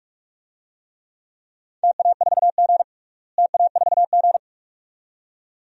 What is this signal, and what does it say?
Morse code, heard as a keyed tone, about 46 wpm, 700 Hz. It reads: TA4G TA4G
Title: DXCC-100 Call Signs - 4 Characters - Repeated 2x - 45wpm